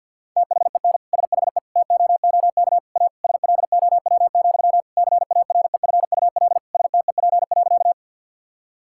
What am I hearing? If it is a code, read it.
Morse code, heard as a keyed tone, about 50 wpm, 700 Hz. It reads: THEN SHE TOOK A SLOW, CAREFUL STEP.
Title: Sentences from Top 1600 Words 50wpm